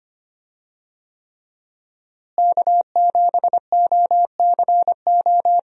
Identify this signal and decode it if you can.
Morse code, heard as a keyed tone, about 25 wpm, 700 Hz. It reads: K7OCO
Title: USA Call Signs - 5 Characters - Repeated 3x - 25wpm